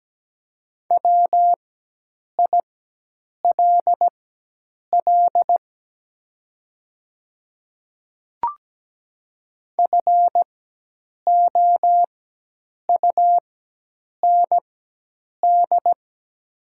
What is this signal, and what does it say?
Morse code, heard as a keyed tone, about 17 wpm, spaced as for 8 wpm, 700 Hz. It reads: WILL FOUND
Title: 4x Character Spacing - Top 500 Words - 17wpm